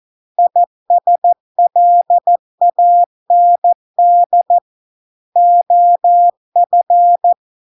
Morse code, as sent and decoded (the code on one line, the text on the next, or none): .. ... .-.. .- -. -.. / --- ..-.
ISLAND OF